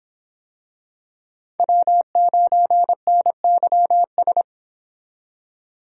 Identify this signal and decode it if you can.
Morse code, heard as a keyed tone, about 25 wpm, 700 Hz. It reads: W9NYS